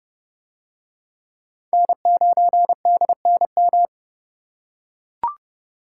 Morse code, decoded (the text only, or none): N9DNM